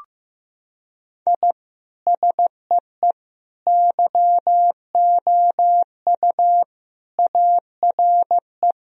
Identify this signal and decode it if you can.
Morse code, heard as a keyed tone, about 16 wpm, 700 Hz. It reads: I SEE YOU ARE
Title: Sentences from Top 1000 Words - Review Part 3 - 15wpm